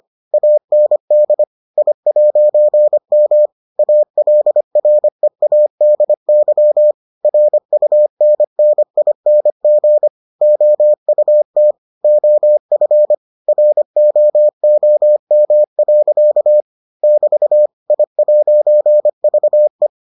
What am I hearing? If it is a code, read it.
Morse code, heard as a keyed tone, about 25 wpm, 600 Hz. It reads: AND I'M ALREADY RUNNING OUT OF ROOM. <BT> I'VE